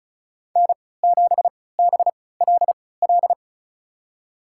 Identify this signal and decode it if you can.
Morse code, heard as a keyed tone, about 36 wpm, spaced as for 20 wpm, 690 Hz. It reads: N7BLL
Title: USA Call Signs - 3x Character Spacing - 35wpm